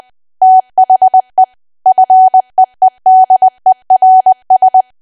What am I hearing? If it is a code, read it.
Morse code, heard as a keyed tone, about 20 wpm, 750 Hz. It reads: THE FEEDERS
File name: cwa / W1AW-Level3 / 20WPM